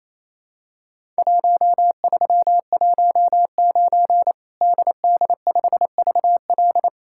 Morse code, heard as a keyed tone, about 28 wpm, 700 Hz. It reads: 1319 DD5VL
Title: Worked All Europe DX Contest - QTC Exchange - 28wpm